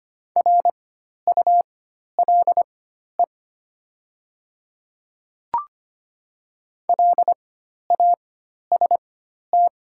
Morse code, heard as a keyed tone, about 25 wpm, spaced as for 12 wpm, 700 Hz. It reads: RULE LAST
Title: 4x Character Spacing - Top 500 Words - 25wpm